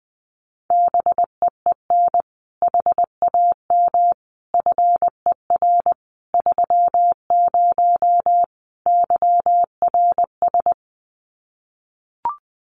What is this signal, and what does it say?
Morse code, heard as a keyed tone, about 20 wpm, 700 Hz. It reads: BEEN HAM FER 30 YRS
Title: Common QSO Phrases Encore 20wpm